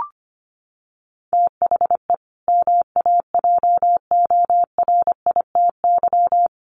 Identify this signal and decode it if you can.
Morse code, heard as a keyed tone, about 25 wpm, 700 Hz. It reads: THE MAJORITY
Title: Sets of 3 Words 25wpm